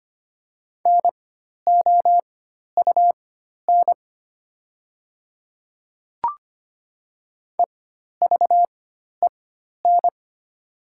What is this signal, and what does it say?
Morse code, heard as a keyed tone, about 25 wpm, spaced as for 12 wpm, 700 Hz. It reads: NOUN EVEN
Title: 4x Character Spacing - Top 500 Words - 25wpm